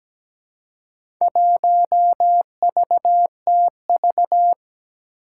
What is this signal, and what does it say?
Morse code, heard as a keyed tone, about 17 wpm, 700 Hz. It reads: 1VTV
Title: Four Challenging Letter-Number Combinations 17wpm